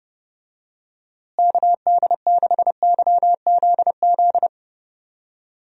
Morse code, heard as a keyed tone, about 29 wpm, 700 Hz. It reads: KD6YZZ
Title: USA Call Signs - 6 Characters - Repeated 3x - 30wpm